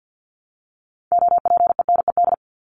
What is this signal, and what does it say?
Morse code, heard as a keyed tone, about 50 wpm, 700 Hz. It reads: OPENED
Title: ED Suffix - Top 100 Words with Base Words 50wpm